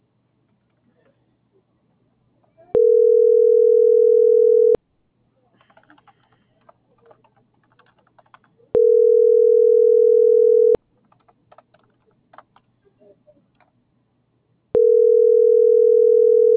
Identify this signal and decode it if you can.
Ringback tone (North America), 440 + 480 Hz.